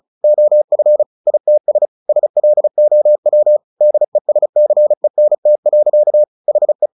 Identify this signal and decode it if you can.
Morse code, heard as a keyed tone, about 34 wpm, 600 Hz. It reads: OF ITS SLOW DESCENT. HE
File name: the old man and the sea 2 35wpm